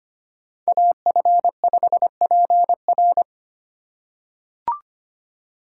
Morse code, heard as a keyed tone, about 25 wpm, 700 Hz. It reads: AF5PR
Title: USA Call Signs - 5 Characters - 3x Character Spacing - 25wpm